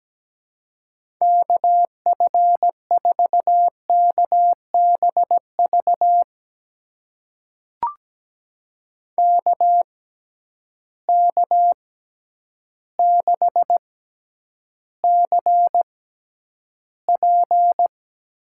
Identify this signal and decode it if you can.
Morse code, heard as a keyed tone, about 17 wpm, spaced as for 9 wpm, 700 Hz. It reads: KF4KBV KK6CP
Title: USA Call Signs - 6 Characters - 6x Character Spacing - 17wpm